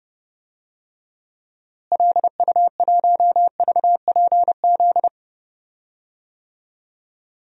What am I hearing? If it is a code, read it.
Morse code, heard as a keyed tone, about 32 wpm, 700 Hz. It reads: LU1VPZ
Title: DXCC-100 Call Signs - 6 Characters - Repeated 2x - 30wpm